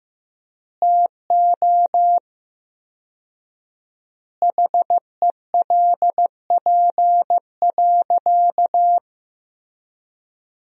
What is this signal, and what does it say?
Morse code, heard as a keyed tone, about 15 wpm, 700 Hz. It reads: TO HELP.
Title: 4x Word-Spacing - Sentences from Top 300 Words - 15wpm